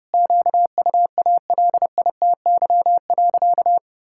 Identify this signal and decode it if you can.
Morse code, heard as a keyed tone, about 30 wpm, 700 Hz. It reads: QUALITY.
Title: Sentences from Top 1000 Words - Review Part 5 - 30wpm